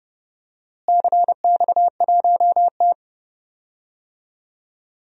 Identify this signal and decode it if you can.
Morse code, heard as a keyed tone, about 30 wpm, 700 Hz. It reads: CX1T
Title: DXCC-100 Call Signs - 4 Characters - Repeated 2x - 30wpm